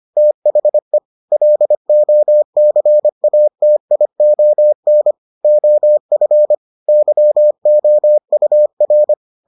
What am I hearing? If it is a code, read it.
Morse code, heard as a keyed tone, about 25 wpm, 600 Hz. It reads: THE LOCATION OF YOUR